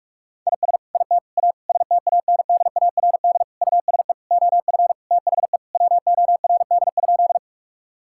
Morse code, heard as a keyed tone, about 42 wpm, 700 Hz. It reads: IS IT A STANDARD USE OF THE WORD?
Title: Sentences from Top 1000 Words - Review Part 6 - 45wpm